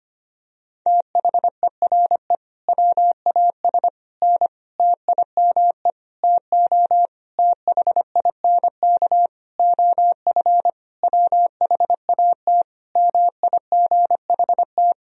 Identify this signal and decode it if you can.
Morse code, heard as a keyed tone, about 25 wpm, 700 Hz. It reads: THERE WAS N TIME TO THINK OF WHAT MIGHT
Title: Sentences from Top 400 Words and Abbreviations 25wpm